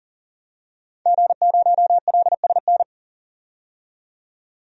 Morse code, heard as a keyed tone, about 40 wpm, 700 Hz. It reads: G0LSN
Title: DXCC-100 Call Signs - 5 Characters - 40wpm